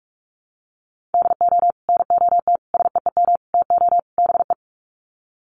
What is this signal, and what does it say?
Morse code, heard as a keyed tone, about 44 wpm, 700 Hz. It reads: DO NOT SEEM TO BE
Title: Sets of 5 Words Encore2 45wpm